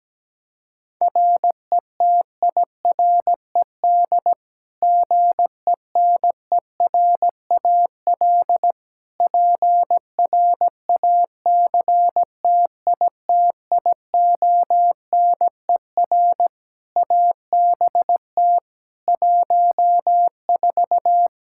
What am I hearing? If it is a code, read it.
Morse code, heard as a keyed tone, about 17 wpm, 700 Hz. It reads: RETIRED GENERAL PRACTITIONER ABT 14
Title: Practice QSO - Exchange 3 - More Info - 17wpm